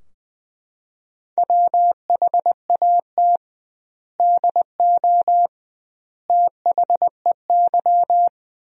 Morse code, heard as a keyed tone, about 21 wpm, 700 Hz. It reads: WHAT DO THEY
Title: 2x Word-Spacing - Sentences from Top 100 Words - 20wpm